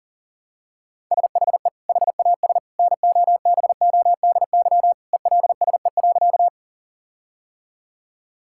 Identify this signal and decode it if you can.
Morse code, heard as a keyed tone, about 40 wpm, 700 Hz. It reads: SHE HAS NOBODY ELSE.
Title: Sentences from Top 1600 Words 40wpm